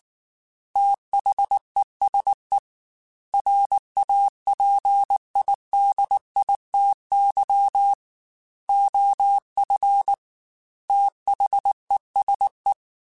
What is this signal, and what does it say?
Morse code, heard as a keyed tone, about 19 wpm, 800 Hz. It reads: THESE RAPIDITY OF THESE
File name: Technical Subjects / Technical Subjects-mp3s / Reactance @18